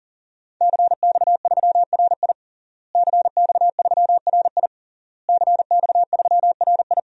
Morse code, heard as a keyed tone, about 37 wpm, 700 Hz. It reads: CX3RI CX3RI CX3RI
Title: DXCC-100 Call Signs - 5 Characters - Repeated 3x - 40wpm